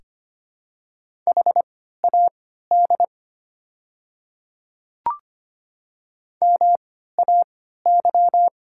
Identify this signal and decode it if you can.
Morse code, heard as a keyed tone, about 25 wpm, spaced as for 14 wpm, 700 Hz. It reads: HAD MAY
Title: 3x Character Spacing - Top 100 Words - 25wpm